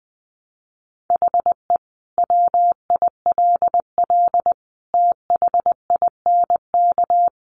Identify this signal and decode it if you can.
Morse code, heard as a keyed tone, about 20 wpm, 700 Hz. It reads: HE WILL THINK
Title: Sentences from Top 1100 Words 20wpm